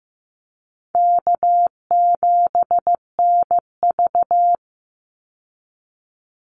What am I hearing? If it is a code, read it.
Morse code, heard as a keyed tone, about 15 wpm, 700 Hz. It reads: K7NV
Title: Easy Call Signs - 15wpm